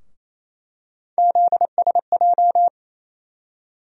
Morse code, heard as a keyed tone, about 28 wpm, 700 Hz. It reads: ZSJ
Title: USA Call Sign Postfixes - 3 Letters 28wpm